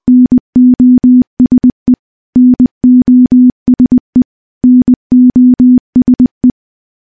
Morse code, heard as a keyed tone, about 20 wpm, 250 Hz. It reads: NOSE NOSE NOSE
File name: morse / morse-training-20WPM